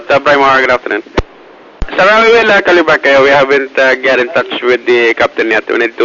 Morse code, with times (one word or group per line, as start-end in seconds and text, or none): none